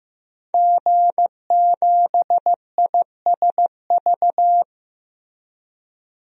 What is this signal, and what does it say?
Morse code, heard as a keyed tone, about 15 wpm, 700 Hz. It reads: G7ISV
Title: DXCC-100 Call Signs - 5 Characters - Repeated 2x - 15wpm